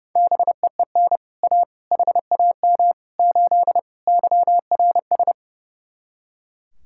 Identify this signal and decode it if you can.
Morse code, heard as a keyed tone, about 34 wpm, 700 Hz. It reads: BEEN A HAM 8 YRS
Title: Speed-Racing Common QSO Phrases - 30wpm